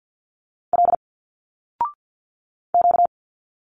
Morse code, heard as a keyed tone, about 50 wpm, 700 Hz. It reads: L Q
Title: Single Letters 50wpm